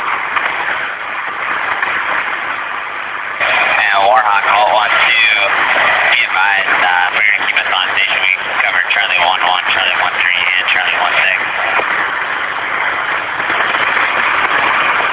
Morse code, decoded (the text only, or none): none